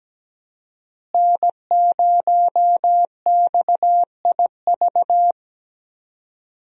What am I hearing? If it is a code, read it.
Morse code, heard as a keyed tone, about 17 wpm, 700 Hz. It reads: N0XIV
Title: USA Call Signs - 5 Characters - Repeated 3x - 17wpm